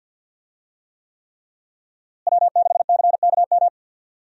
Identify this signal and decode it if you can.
Morse code, heard as a keyed tone, about 50 wpm, 700 Hz. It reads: W6CKM
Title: USA Call Signs - 5 Characters - Repeated 3x - 50wpm